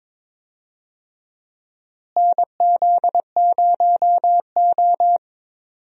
Morse code, heard as a keyed tone, about 22 wpm, 700 Hz. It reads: NZ0O
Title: USA Call Signs - 4 Characters - Repeated 2x - 22wpm